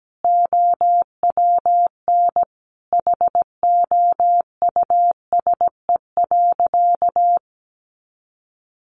Morse code, heard as a keyed tone, about 17 wpm, 700 Hz. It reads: OWN HOUSE.
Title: Sentences from Top 300 Words 17wpm